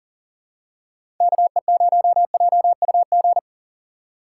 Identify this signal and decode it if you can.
Morse code, heard as a keyed tone, about 40 wpm, 700 Hz. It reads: KE0JUG
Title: USA Call Signs - 6 Characters - 2x Character Spacing - 40wpm